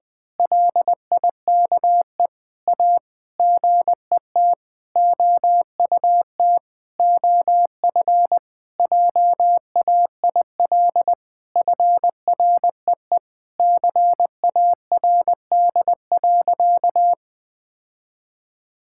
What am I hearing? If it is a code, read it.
Morse code, heard as a keyed tone, about 20 wpm, 700 Hz. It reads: LIKE A GET OUT OF JAIL FREE CARD.
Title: Sentences with 7-Word Phrases - 20wpm